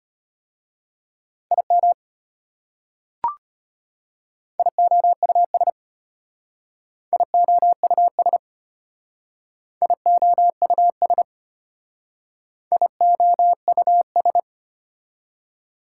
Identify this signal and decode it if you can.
Morse code, heard as a keyed tone, about 33 wpm, 700 Hz. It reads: IM IOUS IOUS IOUS IOUS